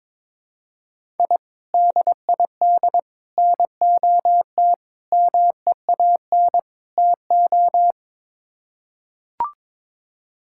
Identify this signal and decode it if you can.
Morse code, heard as a keyed tone, about 22 wpm, 700 Hz. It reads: I DID NOT MEAN TO
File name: practice / 22wpm / Build A Phrase - 5-Word Phrases - 22wpm